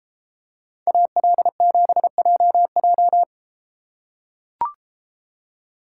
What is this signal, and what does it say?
Morse code, heard as a keyed tone, about 32 wpm, 700 Hz. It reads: AL7JJ